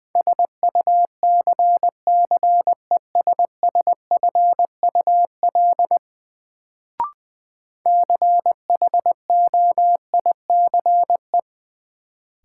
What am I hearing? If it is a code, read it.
Morse code, heard as a keyed tone, about 20 wpm, 700 Hz. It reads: SUCCESSFUL CHOICE
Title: Sentences from Top 900 Words - New Word Drill - 20wpm